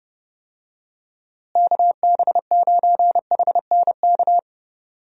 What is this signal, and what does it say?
Morse code, heard as a keyed tone, about 29 wpm, 700 Hz. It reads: KB9HNK
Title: USA Call Signs - 6 Characters - Repeated 3x - 30wpm